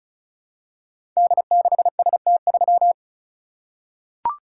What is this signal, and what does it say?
Morse code, heard as a keyed tone, about 36 wpm, 700 Hz. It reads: D6ST3
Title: Code Groups - 5 Letter:Numbers - 35wpm